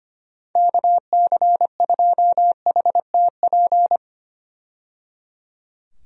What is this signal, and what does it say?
Morse code, heard as a keyed tone, about 25 wpm, 700 Hz. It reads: KC2HTP
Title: USA Call Signs - 6 Characters - 25wpm